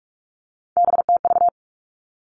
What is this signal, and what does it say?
Morse code, heard as a keyed tone, about 46 wpm, 700 Hz. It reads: BTV
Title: Three Challenging Letter-Number Combinations 45wpm